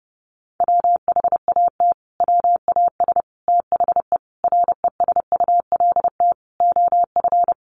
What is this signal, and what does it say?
Morse code, heard as a keyed tone, about 30 wpm, 700 Hz. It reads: WHAT WAS THE RESULT OF
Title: Sentences from Top 1500 Words 30wpm